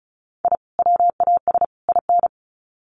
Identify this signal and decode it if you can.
Morse code, heard as a keyed tone, about 34 wpm, 700 Hz. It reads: I WAS IN
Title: Sets of 3 Words 35wpm